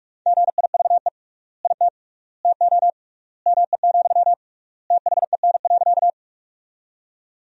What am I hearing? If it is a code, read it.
Morse code, heard as a keyed tone, about 42 wpm, 700 Hz. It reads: GIVE IT TO ME, THEN.
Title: 3x Word-Spacing - Sentences from Top 200 Words - 45wpm